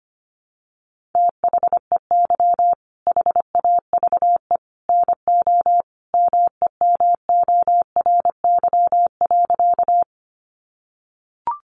Morse code, heard as a keyed tone, about 25 wpm, 700 Hz. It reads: THEY HAVE NO MEMORY.
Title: Sentences from Top 1300 Words 25wpm